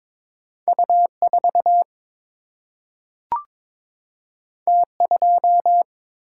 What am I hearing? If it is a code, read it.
Morse code, heard as a keyed tone, about 22 wpm, 700 Hz. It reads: U 4 T 2
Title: Single-Letter Single-Number 22wpm